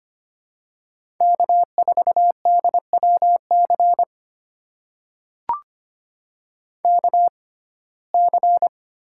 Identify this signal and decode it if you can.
Morse code, heard as a keyed tone, about 25 wpm, spaced as for 18 wpm, 700 Hz. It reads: K4DWC KC
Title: USA Call Signs - 5 Characters - 6x Character Spacing - 25wpm